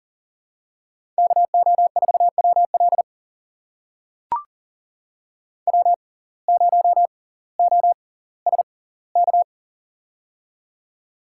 Morse code, heard as a keyed tone, about 39 wpm, spaced as for 20 wpm, 690 Hz. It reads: KO4WL W0OSK
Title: USA Call Signs - 5 Characters - 6x Character Spacing - 40wpm